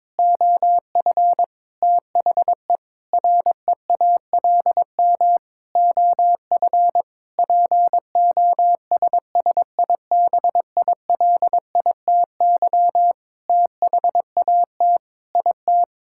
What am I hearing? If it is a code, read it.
Morse code, heard as a keyed tone, about 22 wpm, 700 Hz. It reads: OF THE REALM OF POSSIBILITY THAT IT
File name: Sentences with 9-Word Phrases - 22wpm